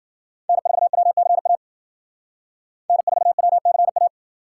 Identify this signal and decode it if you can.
Morse code, heard as a keyed tone, about 52 wpm, 700 Hz. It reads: N4WCA N4WCA